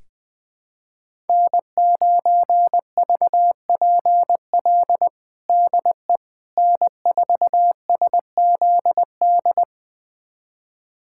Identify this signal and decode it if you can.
Morse code, heard as a keyed tone, about 21 wpm, 700 Hz. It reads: N9VPL DE N4SZD
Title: USA Callsign DE Callsign - 5 Characters - 20wpm